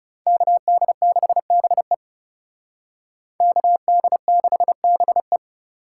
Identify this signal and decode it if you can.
Morse code, heard as a keyed tone, about 34 wpm, 700 Hz. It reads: KD6BE KD6BE
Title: Speed Racing - USA 2x2 Call Signs 30wpm